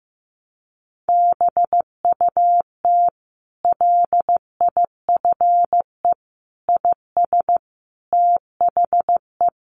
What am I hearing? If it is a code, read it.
Morse code, heard as a keyed tone, about 15 wpm, 700 Hz. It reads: BUT LIFE IS THE
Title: Sentences from Top 900 Words 15wpm